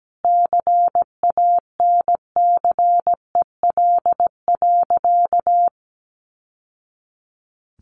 Morse code, heard as a keyed tone, about 17 wpm, 700 Hz. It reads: CANCEL.
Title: Sentences from Top 1200 Words 17wpm